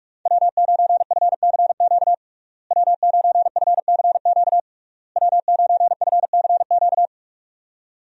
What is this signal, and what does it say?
Morse code, heard as a keyed tone, about 46 wpm, 700 Hz. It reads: W9FCQ W9FCQ W9FCQ
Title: USA Call Signs - 5 Characters - Repeated 3x - 45wpm